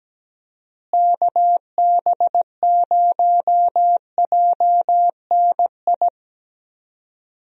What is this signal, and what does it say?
Morse code, heard as a keyed tone, about 17 wpm, 700 Hz. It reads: KB0JNI
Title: USA Call Signs - 6 Characters - Repeated 2x - 17wpm